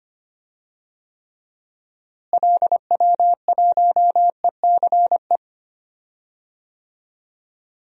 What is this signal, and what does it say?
Morse code, heard as a keyed tone, about 25 wpm, 700 Hz. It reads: LW1ECE
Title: DXCC-100 Call Signs - 6 Characters - Repeated 3x - 25wpm